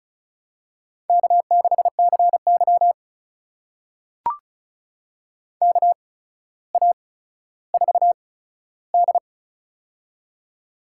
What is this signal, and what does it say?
Morse code, heard as a keyed tone, about 34 wpm, spaced as for 15 wpm, 700 Hz. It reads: K6CY KA4D
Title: USA Call Signs - 4 Characters - 8x Character Spacing - 35wpm